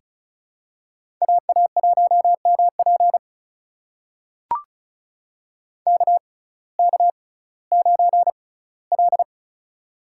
Morse code, heard as a keyed tone, about 34 wpm, spaced as for 19 wpm, 700 Hz. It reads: AA1MP KK9L
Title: USA Call Signs - 6x Character Spacing - 35wpm